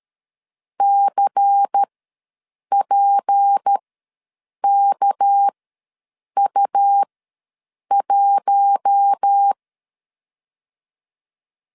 Morse code, heard as a keyed tone, about 13 wpm, spaced as for 7 wpm, 800 Hz. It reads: CPKU1
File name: LEZIONI / 8 YJ45 / 8 Y ws 12wpm 3409